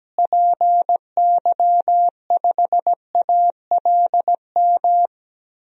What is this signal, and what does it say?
Morse code, heard as a keyed tone, about 17 wpm, 700 Hz. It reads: PY5ALM
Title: DXCC-100 Call Signs - 6 Characters - Repeated 3x - 17wpm